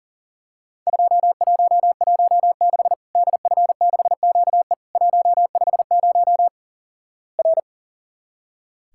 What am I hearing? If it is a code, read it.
Morse code, heard as a keyed tone, about 37 wpm, 700 Hz. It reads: 2116 DF6QE 150 R
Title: Worked All Europe DX Contest - QTC Exchange - 40wpm